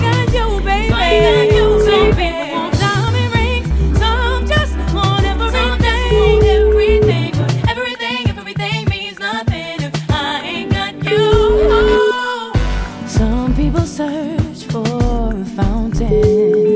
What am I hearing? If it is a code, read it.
Ringback tone (Europe, 425 Hz), with noise.